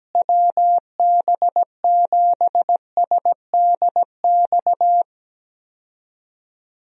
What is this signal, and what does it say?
Morse code, heard as a keyed tone, about 17 wpm, 700 Hz. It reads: WB7SDX
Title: USA Call Signs - 6 Characters - Repeated 3x - 17wpm